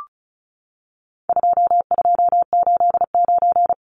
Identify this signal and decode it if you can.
Morse code, heard as a keyed tone, about 34 wpm, 700 Hz. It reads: 2289
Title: Numbers - 4-Digit 35wpm